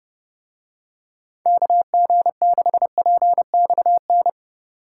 Morse code, heard as a keyed tone, about 30 wpm, 700 Hz. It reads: KG6PXN